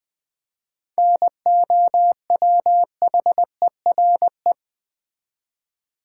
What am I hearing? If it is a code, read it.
Morse code, heard as a keyed tone, about 20 wpm, 700 Hz. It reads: NOWHERE
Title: Seven Letter Words - 20wpm